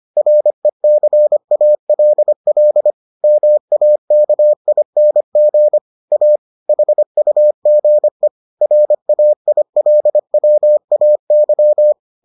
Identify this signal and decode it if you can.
Morse code, heard as a keyed tone, about 25 wpm, 600 Hz. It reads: RECALL MAKING A HUGE RAILWAY